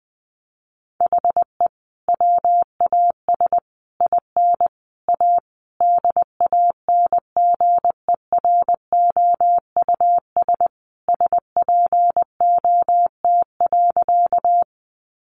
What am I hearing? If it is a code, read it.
Morse code, heard as a keyed tone, about 20 wpm, 700 Hz. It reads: HE WAS IN A DANGEROUS SPOT.
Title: Sentences from Top 1200 Words 20wpm